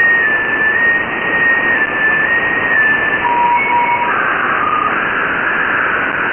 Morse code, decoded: none